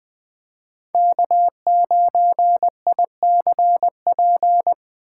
Morse code, heard as a keyed tone, about 20 wpm, 700 Hz. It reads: K9ICP
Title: 2023 POTA Hunter Call Signs - 20wpm